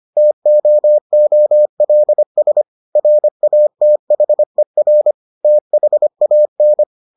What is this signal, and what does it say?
Morse code, heard as a keyed tone, about 25 wpm, 600 Hz. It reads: TOOLS RATHER THAN